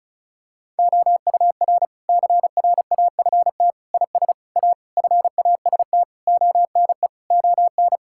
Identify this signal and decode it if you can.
Morse code, heard as a keyed tone, about 36 wpm, 700 Hz. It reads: OUR CRAFT IS A FAST ONE ON
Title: Sentences from Top 1000 Words - Review Part 6 - 35wpm